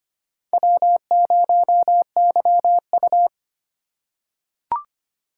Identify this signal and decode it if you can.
Morse code, heard as a keyed tone, about 25 wpm, 700 Hz. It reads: W0YU
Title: USA Call Signs - 3x Character Spacing - 25wpm